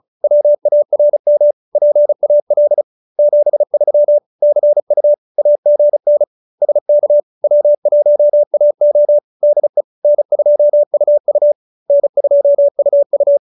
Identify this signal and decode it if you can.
Morse code, heard as a keyed tone, about 34 wpm, 570 Hz. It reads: WARM PAL 73 CU AGN SK W1AO DE N2UU N2UU
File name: QSO313 35